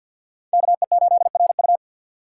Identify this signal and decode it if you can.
Morse code, heard as a keyed tone, about 48 wpm, 700 Hz. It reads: KE8RU
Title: K1USN SST QSO Practice - 50wpm